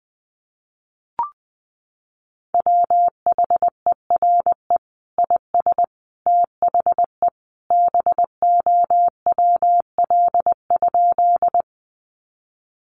Morse code, heard as a keyed tone, about 20 wpm, 700 Hz. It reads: WHERE IS THE BOWL?
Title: Sentences from Top 1200 Words 20wpm